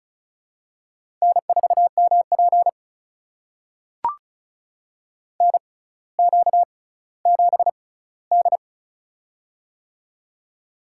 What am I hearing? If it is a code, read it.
Morse code, heard as a keyed tone, about 34 wpm, spaced as for 18 wpm, 700 Hz. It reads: N4MP NQ7D
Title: USA Call Signs - 4 Characters - 6x Character Spacing - 35wpm